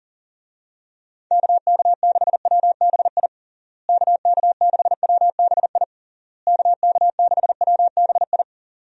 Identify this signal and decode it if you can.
Morse code, heard as a keyed tone, about 40 wpm, 700 Hz. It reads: KK6WBI KK6WBI KK6WBI
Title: USA Call Signs - 6 Characters - Repeated 3x - 40wpm